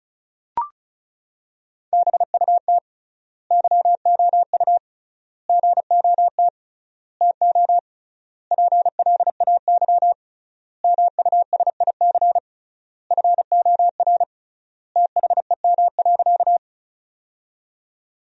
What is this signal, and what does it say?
Morse code, heard as a keyed tone, about 36 wpm, 700 Hz. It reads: BUT YOU GOT TO PLAY MUSIC FOR THEM.
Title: 3x Word-Spacing - Sentences from Top 300 Words - 35wpm